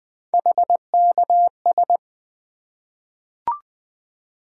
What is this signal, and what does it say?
Morse code, heard as a keyed tone, about 20 wpm, 700 Hz. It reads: HKS
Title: Three Letter-Number Combinations - 20wpm